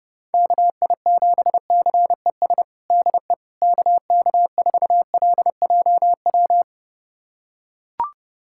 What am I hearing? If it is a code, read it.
Morse code, heard as a keyed tone, about 30 wpm, 700 Hz. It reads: KI7CES DE KK4LJW
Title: USA Callsign DE Callsign 30wpm